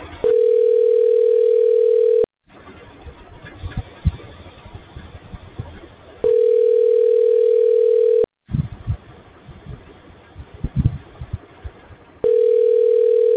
Ringback tone (North America), 440 + 480 Hz.